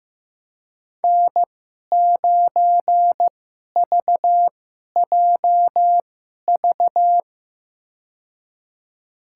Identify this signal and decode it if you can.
Morse code, heard as a keyed tone, about 15 wpm, spaced as for 11 wpm, 700 Hz. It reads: N9VJV
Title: USA Call Signs - 2x Character Spacing - 15wpm